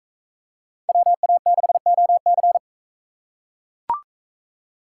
Morse code, heard as a keyed tone, about 41 wpm, 700 Hz. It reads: WA6OC